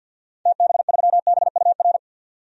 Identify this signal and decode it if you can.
Morse code, heard as a keyed tone, about 53 wpm, 700 Hz. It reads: TB3BUR
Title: DXCC-100 Call Signs - 6 Characters - 50wpm